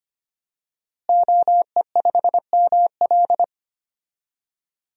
Morse code, heard as a keyed tone, about 25 wpm, 700 Hz. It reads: OE5ML
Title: DXCC-100 Call Signs - 5 Characters - Repeated 3x - 25wpm